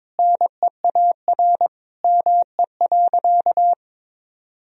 Morse code, heard as a keyed tone, about 22 wpm, 700 Hz. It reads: NEAR ME.
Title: Sentences from Top 1000 Words - Review Part 1 - 22wpm